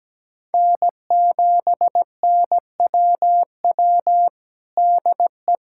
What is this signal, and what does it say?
Morse code, heard as a keyed tone, about 17 wpm, 700 Hz. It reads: N7NWW DE
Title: Practice QSO - Exchange 4 - Ending - 17wpm